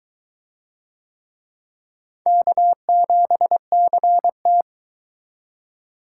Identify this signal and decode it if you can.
Morse code, heard as a keyed tone, about 22 wpm, 700 Hz. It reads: K7CT